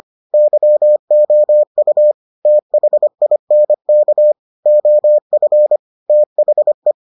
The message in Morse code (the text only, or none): YOU THINK OF THE